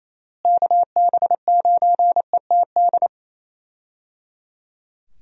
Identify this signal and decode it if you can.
Morse code, heard as a keyed tone, about 28 wpm, 700 Hz. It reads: KB9ETD
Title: USA Call Signs - 6 Characters - Repeated 3x - 28wpm